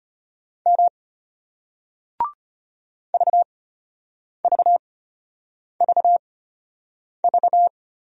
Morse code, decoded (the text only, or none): M VVVV